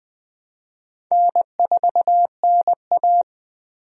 Morse code, heard as a keyed tone, about 20 wpm, 700 Hz. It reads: N4NA
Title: USA Call Signs - 4 Characters - 20wpm